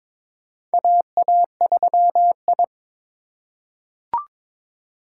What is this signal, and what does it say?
Morse code, heard as a keyed tone, about 22 wpm, 700 Hz. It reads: AA3I
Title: USA Call Signs - 4 Characters - 2x Character Spacing - 22wpm